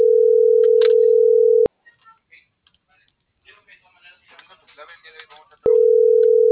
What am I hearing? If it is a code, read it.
Ringback tone (North America), 440 + 480 Hz.